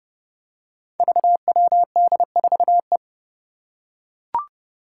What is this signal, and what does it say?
Morse code, heard as a keyed tone, about 29 wpm, 700 Hz. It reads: VWD4E
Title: Five Challenging Letter-Number Combinations 30wpm